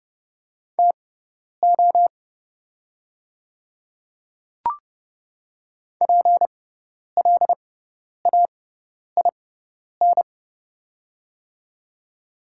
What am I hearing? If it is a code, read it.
Morse code, heard as a keyed tone, about 30 wpm, spaced as for 10 wpm, 690 Hz. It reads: TO PLAIN
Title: 6x Character Spacing - Top 500 Words - 30wpm